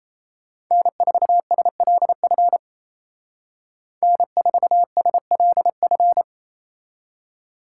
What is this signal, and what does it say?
Morse code, heard as a keyed tone, about 36 wpm, 700 Hz. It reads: N4SLF N4SLF